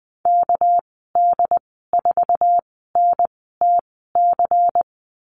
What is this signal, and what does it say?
Morse code, heard as a keyed tone, about 20 wpm, spaced as for 14 wpm, 700 Hz. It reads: KD4NTC
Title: 245 Lesson 39g - Letter z - Call Signs 20wpm